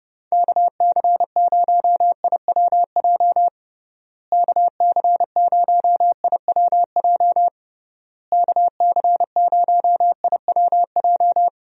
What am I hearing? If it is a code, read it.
Morse code, heard as a keyed tone, about 30 wpm, 700 Hz. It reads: KC0IWJ KC0IWJ KC0IWJ
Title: USA Call Signs - 6 Characters - Repeated 3x - 30wpm